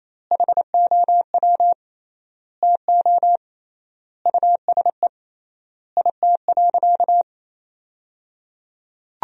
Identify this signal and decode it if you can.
Morse code, heard as a keyed tone, about 28 wpm, 700 Hz. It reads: HOW TO USE IT.
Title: 3x Word-Spacing - Sentences from Top 500 Words - 28wpm